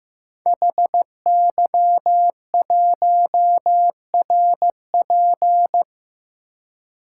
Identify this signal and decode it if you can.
Morse code, heard as a keyed tone, about 15 wpm, 700 Hz. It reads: HY1RP
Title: Code Groups - 5 Letter:Numbers - 15wpm